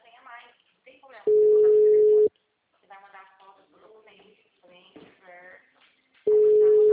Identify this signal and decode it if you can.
Ringback tone (Europe, 425 Hz), with speech.